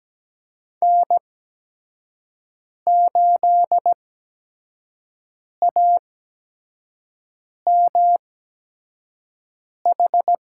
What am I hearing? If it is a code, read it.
Morse code, heard as a keyed tone, about 17 wpm, spaced as for 5 wpm, 700 Hz. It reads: N8AMH
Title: USA Call Signs - 5 Characters - 8x Character Spacing - 17wpm